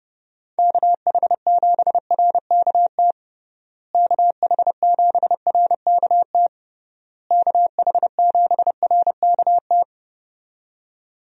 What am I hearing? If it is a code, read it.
Morse code, heard as a keyed tone, about 30 wpm, 700 Hz. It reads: KH7RKT KH7RKT KH7RKT
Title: USA Call Signs - 6 Characters - Repeated 3x - 30wpm